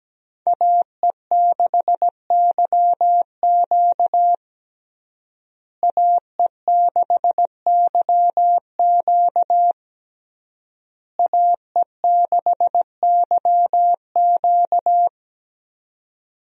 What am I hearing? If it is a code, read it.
Morse code, heard as a keyed tone, about 17 wpm, 700 Hz. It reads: AE6YQ AE6YQ AE6YQ
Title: USA Call Signs - 5 Characters - Repeated 3x - 17wpm